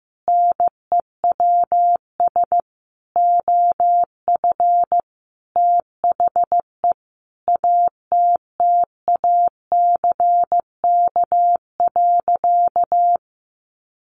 Morse code, heard as a keyed tone, about 15 wpm, 700 Hz. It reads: NEWS OF THE ATTACK.
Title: Sentences from Top 1500 Words 15wpm